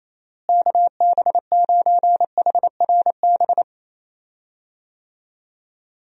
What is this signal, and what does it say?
Morse code, heard as a keyed tone, about 28 wpm, 700 Hz. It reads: KB9HRB
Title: USA Call Signs - 6 Characters - Repeated 2x - 28wpm